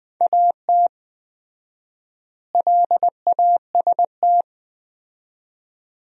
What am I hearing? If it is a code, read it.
Morse code, heard as a keyed tone, about 20 wpm, 700 Hz. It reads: AT LAST
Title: 4x Word-Spacing - Sentences from Top 300 Words - 20wpm